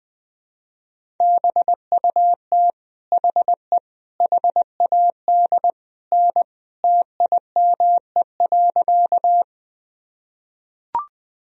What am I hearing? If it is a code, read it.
Morse code, heard as a keyed tone, about 21 wpm, 700 Hz. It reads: BUT HE HAD N TIME.
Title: Sentences from Top 100 Words and Abbreviations 20wpm